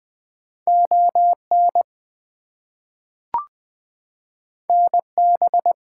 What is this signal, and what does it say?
Morse code, heard as a keyed tone, about 21 wpm, 700 Hz. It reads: ON NB
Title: US States and Canadian Provinces 20wpm